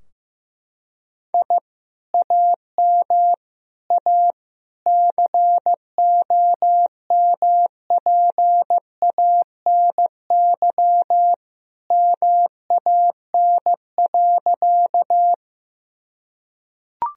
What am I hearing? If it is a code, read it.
Morse code, heard as a keyed tone, about 16 wpm, 700 Hz. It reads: I AM A COMPANY MAN.
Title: Sentences from Top 1000 Words - Review Part 3 - 15wpm